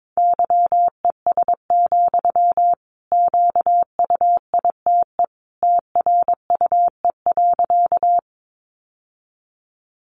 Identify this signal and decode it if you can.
Morse code, heard as a keyed tone, about 22 wpm, 700 Hz. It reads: YES, QUITE TRUE.
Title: Sentences from Top 700 Words 22wpm